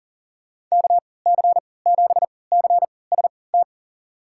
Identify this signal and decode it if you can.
Morse code, heard as a keyed tone, about 40 wpm, spaced as for 23 wpm, 700 Hz. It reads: KC7CST
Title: USA Call Signs - 6 Characters - 3x Character Spacing - 40wpm